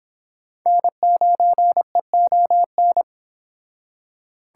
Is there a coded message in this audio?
Yes: Morse code from 0.7 to 3.0 s, heard as a keyed tone, about 25 wpm, 700 Hz. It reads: N9EON